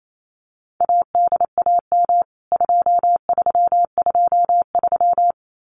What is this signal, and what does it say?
Morse code, heard as a keyed tone, about 28 wpm, 700 Hz. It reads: ADAM 2323
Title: CWT Exchange 28wpm